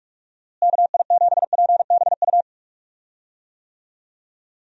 Morse code, heard as a keyed tone, about 44 wpm, 700 Hz. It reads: KI7PBU
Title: USA Call Signs - 6 Characters - Repeated 3x - 45wpm